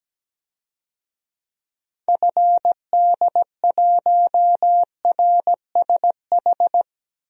Morse code, heard as a keyed tone, about 17 wpm, 700 Hz. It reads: FD1RSH
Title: DXCC-100 Call Signs - 6 Characters - 17wpm